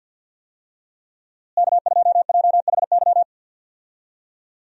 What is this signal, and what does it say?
Morse code, heard as a keyed tone, about 48 wpm, 700 Hz. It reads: K2JHY
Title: USA Call Signs - 5 Characters - Repeated 2x - 50wpm